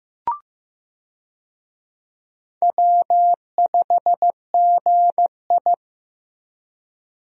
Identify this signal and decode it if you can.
Morse code, heard as a keyed tone, about 15 wpm, 700 Hz. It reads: W5GI
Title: USA Call Signs - 4 Characters - 15wpm